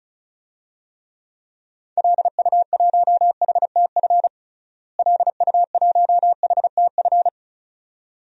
Morse code, heard as a keyed tone, about 36 wpm, 700 Hz. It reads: LU1HTF LU1HTF
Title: DXCC-100 Call Signs - 6 Characters - Repeated 2x - 35wpm